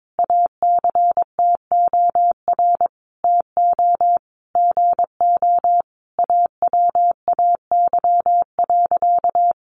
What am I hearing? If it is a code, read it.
Morse code, heard as a keyed tone, about 22 wpm, 700 Hz. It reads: ACTOR TO GO AWAY.
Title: Sentences from Top 1500 Words 22wpm